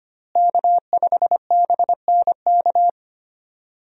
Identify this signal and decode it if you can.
Morse code, heard as a keyed tone, about 25 wpm, 700 Hz. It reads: K5BNK
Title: USA Call Signs - 5 Characters - Repeated 3x - 25wpm